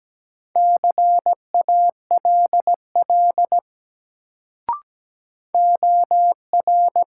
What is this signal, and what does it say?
Morse code, heard as a keyed tone, about 17 wpm, 700 Hz. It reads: CALL OR
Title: Sentences from Top 100 Words - New Word Drill - 17wpm